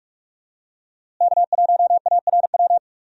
Morse code, heard as a keyed tone, about 45 wpm, 700 Hz. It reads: K1ARW
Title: USA Call Signs - 5 Characters - Repeated 3x - 45wpm